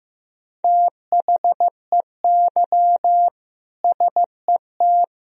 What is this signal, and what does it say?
Morse code, heard as a keyed tone, about 15 wpm, 700 Hz. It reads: THEY SET
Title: Sentences from Top 1000 Words - Review Part 2 - 15wpm